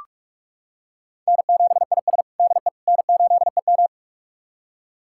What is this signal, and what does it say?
Morse code, heard as a keyed tone, about 46 wpm, 700 Hz. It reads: N7IS DE N8EM
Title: USA Callsign DE Callsign - 4 Characters - 45wpm